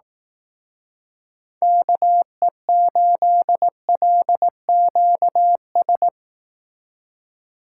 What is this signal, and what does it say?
Morse code, heard as a keyed tone, about 18 wpm, 700 Hz. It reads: KE8LQS
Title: Speed Racing - USA 2x3 Call Signs 15wpm